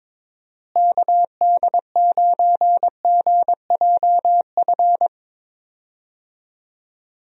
Morse code, heard as a keyed tone, about 21 wpm, 700 Hz. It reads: KD9GJF
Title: USA Call Signs - 6 Characters - Repeated 2x - 22wpm